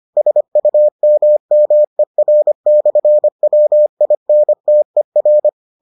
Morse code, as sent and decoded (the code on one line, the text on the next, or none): ... ..- -- -- . .-. -..-. .-- .. -. - . .-.
SUMMER/WINTER